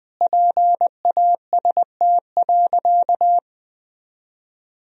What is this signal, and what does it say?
Morse code, heard as a keyed tone, about 21 wpm, 700 Hz. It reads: PAST.
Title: Sentences from Top 1000 Words - Review Part 3 - 20wpm